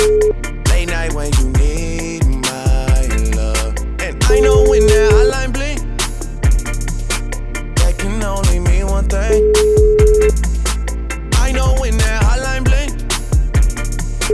Ringback tone (Europe, 425 Hz), with noise.